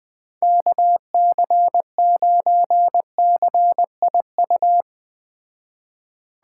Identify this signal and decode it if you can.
Morse code, heard as a keyed tone, about 20 wpm, 700 Hz. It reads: KC9CIU
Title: Speed Racing - USA 2x3 Call Signs 17wpm